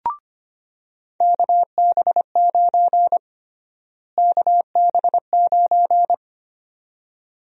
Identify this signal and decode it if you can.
Morse code, heard as a keyed tone, about 25 wpm, 700 Hz. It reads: KB9 KB9
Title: USA Call Sign Prefixes - Repeated 2x - 25wpm